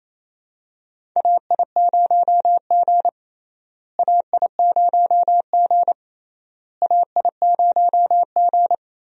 Morse code, heard as a keyed tone, about 28 wpm, 700 Hz. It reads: AI0G AI0G AI0G
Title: USA Call Signs - 4 Characters - Repeated 3x - 28wpm